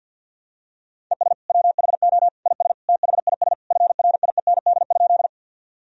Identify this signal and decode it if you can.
Morse code, heard as a keyed tone, about 52 wpm, 700 Hz. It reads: ES WHO IS THIS FRIEND?
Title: Sentences from Top 300 Words and Abbreviations 50wpm